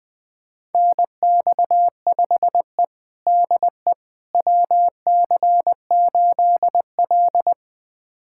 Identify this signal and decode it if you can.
Morse code, heard as a keyed tone, about 21 wpm, 700 Hz. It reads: NX5E DE WC8L
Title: USA Callsign DE Callsign - 4 Characters - 20wpm